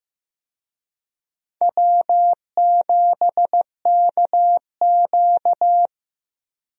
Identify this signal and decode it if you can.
Morse code, heard as a keyed tone, about 15 wpm, 700 Hz. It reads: W7KQ
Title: USA Call Signs - 4 Characters - Repeated 3x - 15wpm